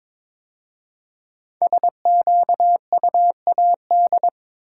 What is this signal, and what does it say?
Morse code, heard as a keyed tone, about 22 wpm, 700 Hz. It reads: SQUAD
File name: practice / 22wpm / 5 Letter Words - First 3 Letters Given - 22wpm